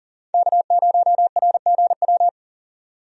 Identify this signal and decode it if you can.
Morse code, heard as a keyed tone, about 40 wpm, 700 Hz. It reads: K0RGW
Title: USA Call Signs - 5 Characters - 40wpm